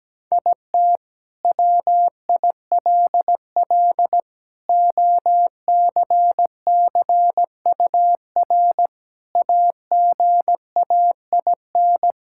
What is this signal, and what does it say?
Morse code, heard as a keyed tone, about 17 wpm, 700 Hz. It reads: IT WILL OCCUR AGAIN
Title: Sentences from Top 1000 Words - Review Part 6 - 17wpm